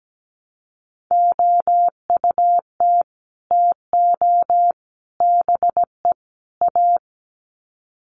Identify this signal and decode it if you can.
Morse code, heard as a keyed tone, about 17 wpm, 700 Hz. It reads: OUT TO BE A
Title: Sets of 4 Words 17wpm